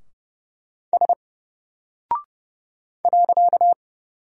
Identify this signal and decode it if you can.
Morse code, heard as a keyed tone, about 30 wpm, 700 Hz. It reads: S .
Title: Single Character (Letter:Number:Punctuation:Prosign) Alt 30wpm